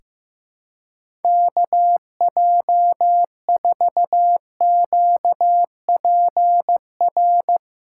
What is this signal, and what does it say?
Morse code, heard as a keyed tone, about 15 wpm, 700 Hz. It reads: KJ4QPR
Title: USA Callsign DE Callsign 15wpm